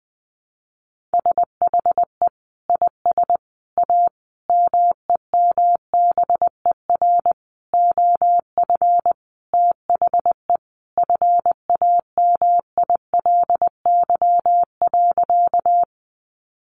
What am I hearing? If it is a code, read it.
Morse code, heard as a keyed tone, about 20 wpm, 700 Hz. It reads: SHE IS A MEMBER OF THE FAMILY.
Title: Sentences from Top 1100 Words 20wpm